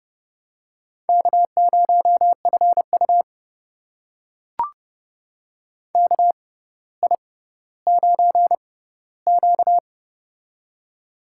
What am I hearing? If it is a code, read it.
Morse code, heard as a keyed tone, about 30 wpm, spaced as for 15 wpm, 700 Hz. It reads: K0FU KI9Q
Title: USA Call Signs - 4 Characters - 6x Character Spacing - 30wpm